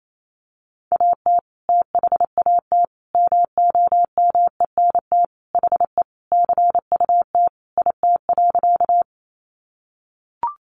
Morse code, heard as a keyed tone, about 28 wpm, 700 Hz. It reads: AT THAT MOMENT HE CUT IT.
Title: Sentences from Top 1000 Words 28wpm